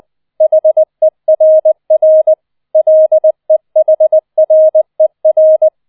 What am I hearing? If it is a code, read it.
Morse code, heard as a keyed tone, about 20 wpm, 630 Hz. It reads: HERR LEHRER
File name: 007 Letzter Streich 20wpm